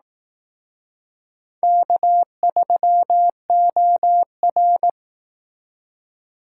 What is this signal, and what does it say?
Morse code, heard as a keyed tone, about 18 wpm, 700 Hz. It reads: K3OR
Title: Speed Racing - USA 1x2 Call Signs 15wpm